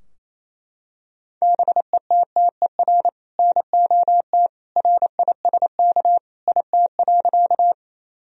Morse code, heard as a keyed tone, about 28 wpm, 700 Hz. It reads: BETTER NOT RISK IT.
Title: Sentences from Top 1000 Words - Review Part 4 - 28wpm